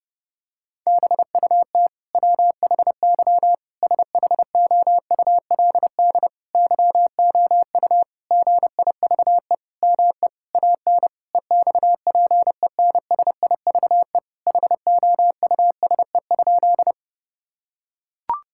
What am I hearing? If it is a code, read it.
Morse code, heard as a keyed tone, about 32 wpm, 700 Hz. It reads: BUT WHY SHOULD YOU GIVE ME AN EXPENSIVE HOUSE?
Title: Sentences from Top 1000 Words - Review Part 6 - 30wpm